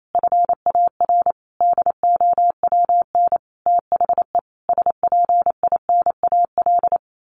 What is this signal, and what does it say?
Morse code, heard as a keyed tone, about 28 wpm, 700 Hz. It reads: FAR DOWN THE SPINAL
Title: A Sound Of Thunder 28wpm